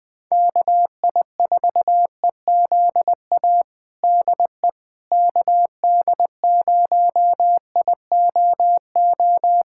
Morse code, heard as a keyed tone, about 20 wpm, 700 Hz. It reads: KI4EZA DE KD0IOO
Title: USA Callsign DE Callsign - 6 Characters - 20wpm